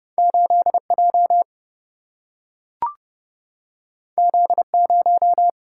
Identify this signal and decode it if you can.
Morse code, heard as a keyed tone, about 30 wpm, 700 Hz. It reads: 8 J Z 0
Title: Single-Letter Single-Number 30wpm